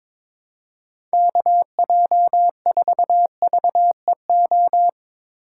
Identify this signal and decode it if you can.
Morse code, heard as a keyed tone, about 21 wpm, 700 Hz. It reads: KJ4VEO
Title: USA Call Signs - 2x Character Spacing - 22wpm